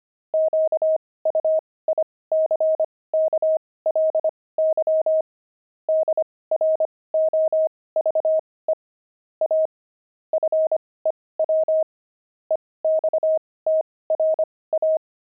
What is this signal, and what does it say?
Morse code, heard as a keyed tone, about 25 wpm, spaced as for 18 wpm, 630 Hz. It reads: QUICKLY DROVE A FEW EXTRA
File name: pangram-fs-18-cs-25-pavement